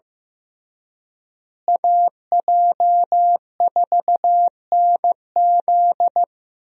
Morse code, heard as a keyed tone, about 16 wpm, 700 Hz. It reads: AJ4NZ
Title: USA Call Signs - 5 Characters - Repeated 2x - 15wpm